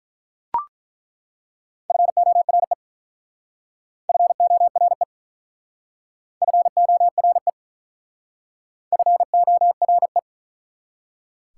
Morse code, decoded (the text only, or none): FORE FORE FORE FORE